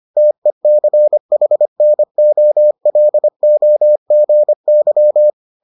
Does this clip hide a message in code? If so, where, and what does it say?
Yes: Morse code from 0.2 to 5.3 s, heard as a keyed tone, about 25 wpm, 600 Hz. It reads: TECHNOLOGY